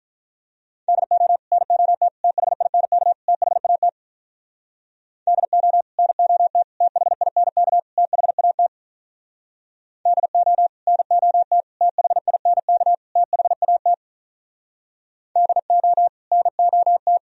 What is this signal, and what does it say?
Morse code, heard as a keyed tone, about 44 wpm, 700 Hz. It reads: DO NOT THINK THAT DO NOT THINK THAT DO NOT THINK THAT DO NOT